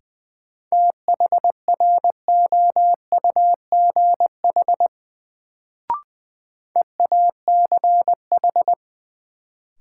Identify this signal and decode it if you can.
Morse code, heard as a keyed tone, about 20 wpm, 700 Hz. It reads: THROUGH EACH
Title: Sets of 2 Words - New Word Drill - 20wpm